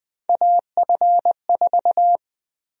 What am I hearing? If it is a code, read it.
Morse code, heard as a keyed tone, about 20 wpm, 700 Hz. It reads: AF4
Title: USA Call Sign Prefixes - 20wpm